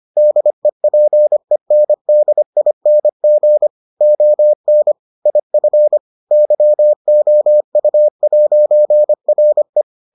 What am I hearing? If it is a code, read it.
Morse code, heard as a keyed tone, about 25 wpm, 600 Hz. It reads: DEPENDING ON IF YOU'RE